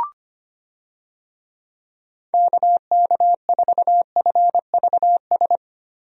Morse code, heard as a keyed tone, about 25 wpm, 700 Hz. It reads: KK4FVS
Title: USA Call Signs - 6 Characters - 25wpm